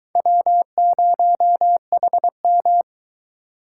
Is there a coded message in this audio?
Yes: Morse code from 0.2 to 2.8 s, heard as a keyed tone, about 22 wpm, 700 Hz. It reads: W0HM